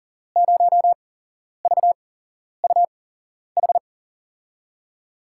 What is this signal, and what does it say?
Morse code, heard as a keyed tone, about 40 wpm, spaced as for 11 wpm, 700 Hz. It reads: 0VUH
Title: USA Call Signs - 6 Characters - 8x Character Spacing - 40wpm